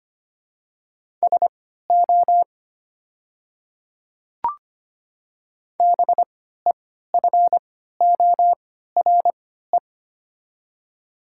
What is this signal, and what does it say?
Morse code, heard as a keyed tone, about 25 wpm, spaced as for 14 wpm, 700 Hz. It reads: SO BEFORE
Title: 3x Character Spacing - Top 200 Words - 25wpm